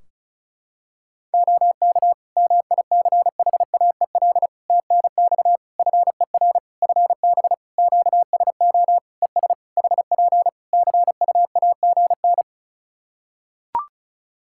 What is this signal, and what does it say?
Morse code, heard as a keyed tone, about 36 wpm, 700 Hz. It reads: OK MICHAEL TNX FER FB QSO ES HP CUAGN
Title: Practice QSO - Exchange 4 - Ending - 35wpm